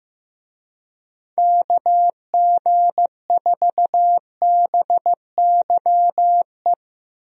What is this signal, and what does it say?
Morse code, heard as a keyed tone, about 15 wpm, 700 Hz. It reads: KG4BYE
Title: USA Call Signs - 6 Characters - Repeated 3x - 15wpm